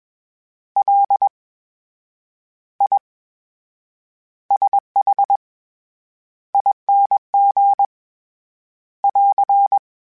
Morse code, heard as a keyed tone, about 22 wpm, spaced as for 9 wpm, 800 Hz. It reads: LISHING <AR>
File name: Special INGs-mp3s / Special INGs-mp3s / VerySlowToFaster Ing @20